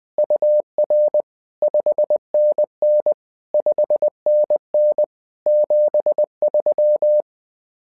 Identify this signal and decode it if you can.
Morse code, heard as a keyed tone, about 20 wpm, 600 Hz. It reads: UR 5NN 5NN 73
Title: SOTA QSO Practice 20wpm